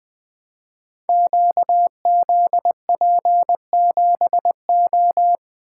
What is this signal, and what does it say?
Morse code, heard as a keyed tone, about 20 wpm, 700 Hz. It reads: QZP7O
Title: Code Groups - 5 Letter:Numbers - 20wpm